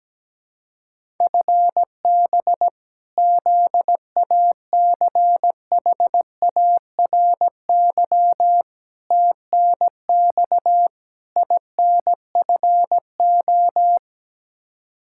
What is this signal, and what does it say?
Morse code, heard as a keyed tone, about 17 wpm, 700 Hz. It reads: FB ZACHARY TNX INFO
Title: Practice QSO - Exchange 3 - More Info - 17wpm